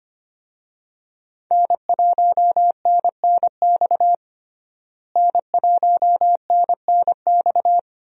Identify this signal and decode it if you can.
Morse code, heard as a keyed tone, about 25 wpm, 700 Hz. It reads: N1NNX N1NNX
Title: USA Call Signs - 5 Characters - Repeated 3x - 25wpm